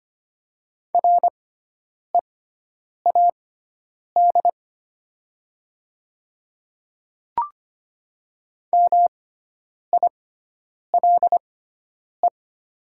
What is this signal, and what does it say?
Morse code, heard as a keyed tone, about 25 wpm, spaced as for 9 wpm, 700 Hz. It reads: READ MILE
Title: 6x Character Spacing - Top 300 Words - 25wpm